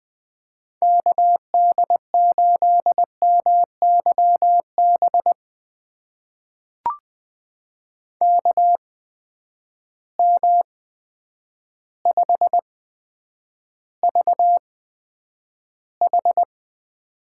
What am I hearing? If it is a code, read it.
Morse code, heard as a keyed tone, about 20 wpm, spaced as for 9 wpm, 700 Hz. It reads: KD8MYB KM5VH
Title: USA Call Signs - 8x Character Spacing - 20wpm